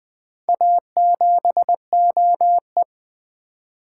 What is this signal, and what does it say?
Morse code, heard as a keyed tone, about 20 wpm, 700 Hz. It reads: A7OE
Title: Four Letter-Number Combinations - 20wpm